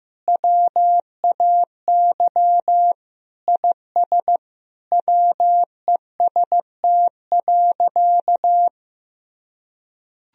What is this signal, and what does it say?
Morse code, heard as a keyed tone, about 15 wpm, 700 Hz. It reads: WAY IS WEST.
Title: Sentences from Top 1000 Words - Review Part 3 - 15wpm